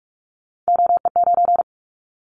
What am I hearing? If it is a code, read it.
Morse code, heard as a keyed tone, about 46 wpm, 700 Hz. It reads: OE9
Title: Callsign Prefixes for Top 50 DXCC - 45wpm